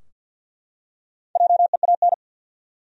Morse code, heard as a keyed tone, about 52 wpm, 700 Hz. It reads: JEAN
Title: USA Common Names - 50wpm